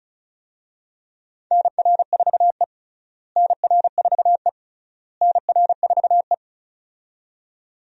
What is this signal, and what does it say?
Morse code, heard as a keyed tone, about 34 wpm, 700 Hz. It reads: NR4E NR4E NR4E
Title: USA Call Signs - 4 Characters - Repeated 3x - 35wpm